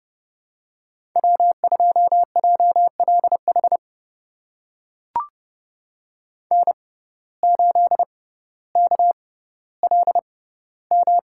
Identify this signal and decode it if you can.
Morse code, heard as a keyed tone, about 30 wpm, spaced as for 15 wpm, 700 Hz. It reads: W2JLH N8KLM
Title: USA Call Signs - 5 Characters - 6x Character Spacing - 30wpm